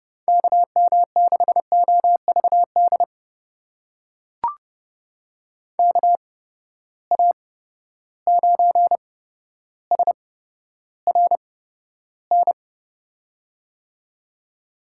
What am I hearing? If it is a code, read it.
Morse code, heard as a keyed tone, about 31 wpm, spaced as for 13 wpm, 700 Hz. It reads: KM6OVD KA9SRN
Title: USA Call Signs - 6 Characters - 8x Character Spacing - 30wpm